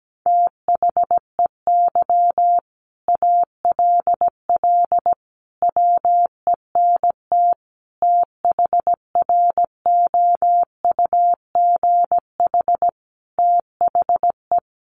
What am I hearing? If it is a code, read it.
Morse code, heard as a keyed tone, about 17 wpm, 700 Hz. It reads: THEY ALL WENT THROUGH THE
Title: Sentences from Top 1300 Words 17wpm